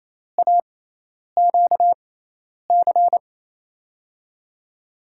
Morse code, heard as a keyed tone, about 28 wpm, spaced as for 10 wpm, 700 Hz. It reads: AQC
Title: USA Call Signs - 6 Characters - 6x Character Spacing - 28wpm